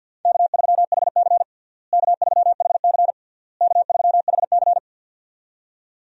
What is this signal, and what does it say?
Morse code, heard as a keyed tone, about 46 wpm, 700 Hz. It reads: K3HC K3HC K3HC
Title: USA Call Signs - 4 Characters - Repeated 3x - 50wpm